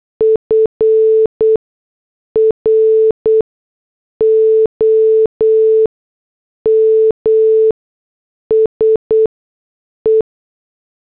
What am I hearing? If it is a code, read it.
Morse code, heard as a keyed tone, about 8 wpm, spaced as for 6 wpm, 440 Hz. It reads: FROM SE
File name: morse / headlines